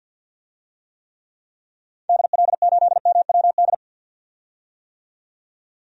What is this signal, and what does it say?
Morse code, heard as a keyed tone, about 51 wpm, 700 Hz. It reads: DL8MWD
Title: DXCC-100 Call Signs - 6 Characters - 50wpm